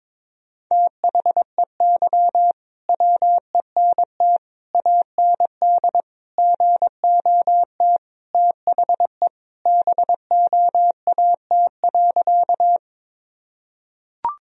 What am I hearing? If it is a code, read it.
Morse code, heard as a keyed tone, about 23 wpm, 700 Hz. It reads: THEY WENT AND GOT THE BOAT.
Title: Sentences from Top 1000 Words - Review Part 3 - 22wpm